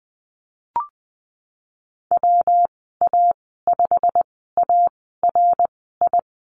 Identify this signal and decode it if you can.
Morse code, heard as a keyed tone, about 20 wpm, spaced as for 15 wpm, 700 Hz. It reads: WA5ARI
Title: 167 Lesson 27g - Number 7 - Call Signs 20wpm